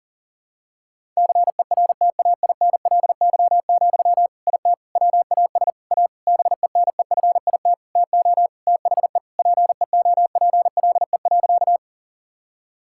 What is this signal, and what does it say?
Morse code, heard as a keyed tone, about 40 wpm, 700 Hz. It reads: CERTAINLY, IT WAS A BENEFIT TO THE PEOPLE.
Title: Sentences from Top 1000 Words - Review Part 6 - 40wpm